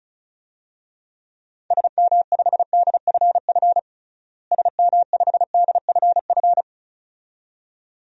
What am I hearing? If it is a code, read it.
Morse code, heard as a keyed tone, about 34 wpm, 700 Hz. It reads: SM5DFF SM5DFF
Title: DXCC-100 Call Signs - 6 Characters - Repeated 2x - 35wpm